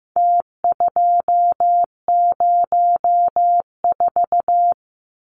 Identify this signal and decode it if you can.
Morse code, heard as a keyed tone, about 15 wpm, 700 Hz. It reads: T204
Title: Numbers - Contest Style - 4-Digit 15wpm